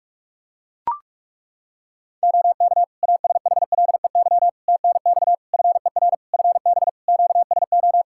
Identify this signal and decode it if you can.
Morse code, heard as a keyed tone, about 41 wpm, 700 Hz. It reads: OK ASHLEY TNX FER FB QSO
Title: Practice QSO - Exchange 4 - Ending - 45wpm